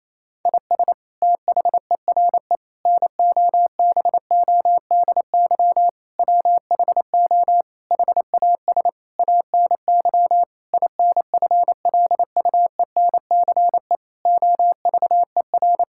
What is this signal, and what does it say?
Morse code, heard as a keyed tone, about 28 wpm, 700 Hz. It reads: IS THERE NOBODY WHO HAS ANY INFLUENCE OVER
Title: Sentences from Top 1600 Words 28wpm